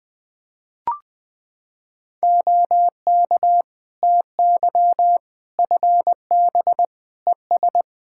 Morse code, heard as a keyed tone, about 20 wpm, 700 Hz. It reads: OK TY FB ES
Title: K1USN SST QSO Practice 20wpm